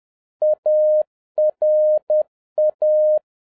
Morse code, heard as a keyed tone, about 10 wpm, 610 Hz. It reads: ARA